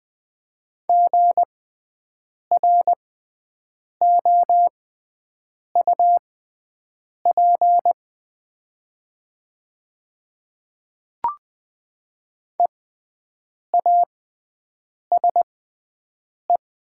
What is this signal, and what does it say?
Morse code, heard as a keyed tone, about 20 wpm, spaced as for 7 wpm, 700 Hz. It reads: GROUP EASE
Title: 6x Character Spacing - Top 300 Words - 20wpm